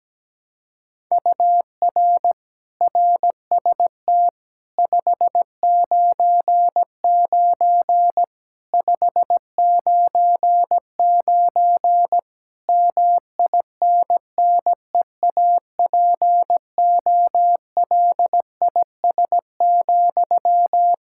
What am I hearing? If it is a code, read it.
Morse code, heard as a keyed tone, about 17 wpm, 700 Hz. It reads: UR RST 599 599 MINNEAPOLIS,
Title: Practice QSO - Exchange 1 - Beginning - 17wpm